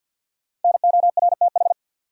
Morse code, heard as a keyed tone, about 53 wpm, 700 Hz. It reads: NORTH
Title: Sets of 5 Words Encore - New Word Drill - 50wpm